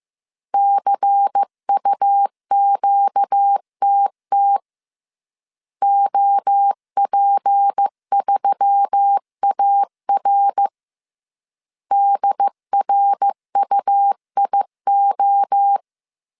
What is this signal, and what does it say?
Morse code, heard as a keyed tone, about 15 wpm, 800 Hz. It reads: CUQTT OP3AR DRUIO
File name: CW / DET / 1 6 1500C S C 14wpm 2951